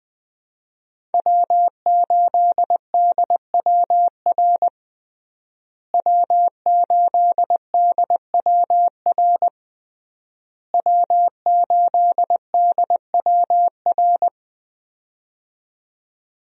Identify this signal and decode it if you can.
Morse code, heard as a keyed tone, about 20 wpm, 700 Hz. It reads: W8DWR W8DWR W8DWR
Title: USA Call Signs - 5 Characters - Repeated 3x - 20wpm